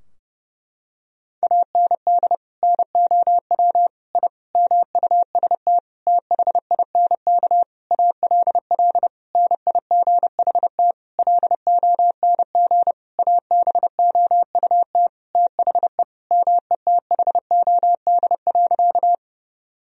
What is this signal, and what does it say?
Morse code, heard as a keyed tone, about 30 wpm, 700 Hz. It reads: AND NOW I MUST THINK ALL NIGHT LONG ABOUT THE METHOD.
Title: Sentences from Top 1000 Words - Review Part 3 - 30wpm